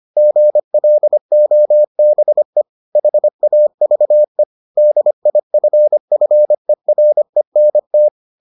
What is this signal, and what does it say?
Morse code, heard as a keyed tone, about 25 wpm, 600 Hz. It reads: GLOBE HAVE DIFFERENT